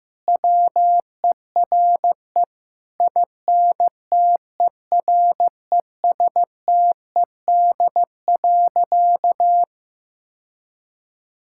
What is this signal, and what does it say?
Morse code, heard as a keyed tone, about 15 wpm, 700 Hz. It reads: WERE INTERESTED.
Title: Sentences from Top 1000 Words - Review Part 6 - 15wpm